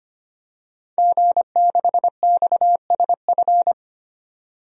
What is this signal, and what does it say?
Morse code, heard as a keyed tone, about 25 wpm, 700 Hz. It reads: G6XSF
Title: DXCC-100 Call Signs - 5 Characters - 25wpm